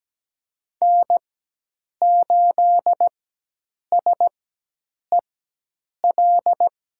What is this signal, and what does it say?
Morse code, heard as a keyed tone, about 17 wpm, spaced as for 8 wpm, 700 Hz. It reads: N8SEL
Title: USA Call Signs - 5 Characters - 4x Character Spacing - 17wpm